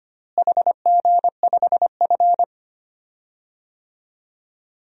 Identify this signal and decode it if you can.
Morse code, heard as a keyed tone, about 25 wpm, 700 Hz. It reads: HG5F
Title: DXCC-100 Call Signs - 4 Characters - Repeated 3x - 25wpm